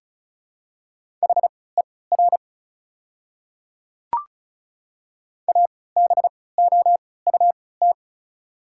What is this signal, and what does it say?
Morse code, heard as a keyed tone, about 36 wpm, spaced as for 20 wpm, 700 Hz. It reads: HER ABOUT
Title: 3x Character Spacing - Top 100 Words - 35wpm